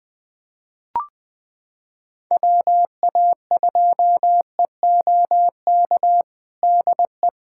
Morse code, heard as a keyed tone, about 21 wpm, 700 Hz. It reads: WA2EOK DE
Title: USA Callsign DE Callsign - 6 Characters - 20wpm